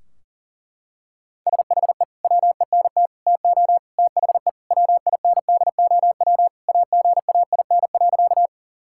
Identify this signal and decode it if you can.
Morse code, heard as a keyed tone, about 40 wpm, 700 Hz. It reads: SHE WENT TO THE WINDOW AGAIN.
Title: Sentences from Top 1600 Words 40wpm